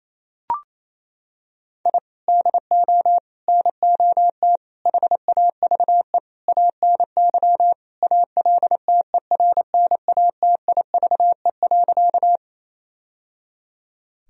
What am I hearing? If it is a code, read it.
Morse code, heard as a keyed tone, about 28 wpm, 700 Hz. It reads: I DO NOT HAVE ANY ALTERNATIVE.
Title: Sentences from Top 1000 Words - Review Part 4 - 28wpm